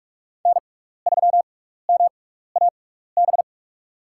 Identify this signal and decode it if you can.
Morse code, heard as a keyed tone, about 47 wpm, spaced as for 16 wpm, 700 Hz. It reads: N3MAB
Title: USA Call Signs - 5 Characters - 6x Character Spacing - 45wpm